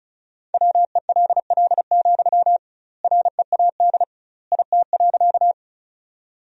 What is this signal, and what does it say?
Morse code, heard as a keyed tone, about 36 wpm, 700 Hz. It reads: WELL, READ IT.
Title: 2x Word-Spacing - Sentences from Top 200 Words - 35wpm